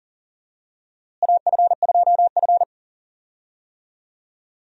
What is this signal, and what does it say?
Morse code, heard as a keyed tone, about 40 wpm, 700 Hz. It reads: AF2F
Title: USA Call Signs - 4 Characters - Repeated 3x - 40wpm